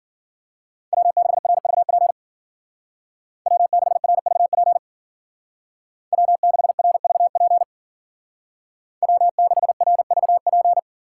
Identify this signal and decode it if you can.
Morse code, heard as a keyed tone, about 52 wpm, 700 Hz. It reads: W6RVP W6RVP W6RVP W6RVP